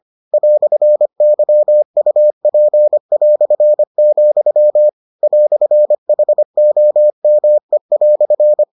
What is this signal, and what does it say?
Morse code, heard as a keyed tone, about 25 wpm, 600 Hz. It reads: "YUP", "HOME"